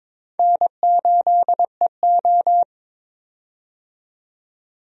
Morse code, heard as a keyed tone, about 22 wpm, 700 Hz. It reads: N8EO
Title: USA Call Signs - 4 Characters - 22wpm